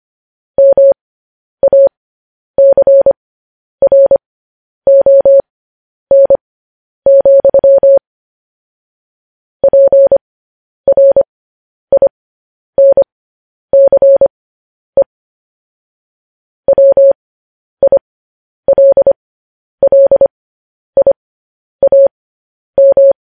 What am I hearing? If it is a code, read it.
Morse code, heard as a keyed tone, about 25 wpm, spaced as for 10 wpm, 550 Hz. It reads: MACRON, PRINCE WILLIAM
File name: podcast / News headlines-1x-25x10